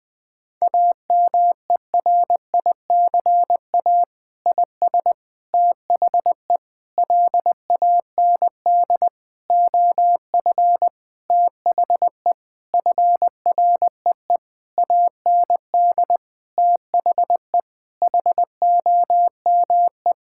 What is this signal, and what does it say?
Morse code, heard as a keyed tone, about 20 wpm, 700 Hz. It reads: AMERICA IS THE LAND OF THE FREE AND THE HOME
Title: Sentences with 8-Word Phrases - Spoken Beginning - 20wpm